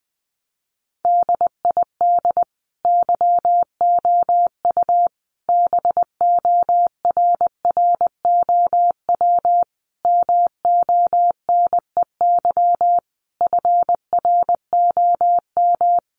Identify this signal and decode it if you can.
Morse code, heard as a keyed tone, about 20 wpm, 700 Hz. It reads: DID YOU BORROW MONEY FROM
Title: Sentences from Top 1500 Words 20wpm